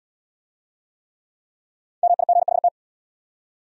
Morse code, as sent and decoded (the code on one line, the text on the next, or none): -- .. --. .... -
MIGHT